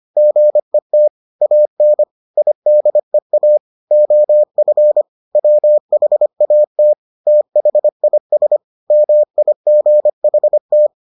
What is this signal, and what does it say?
Morse code, heard as a keyed tone, about 25 wpm, 600 Hz. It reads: GET AN IDEA OF WHAT THIS MIGHT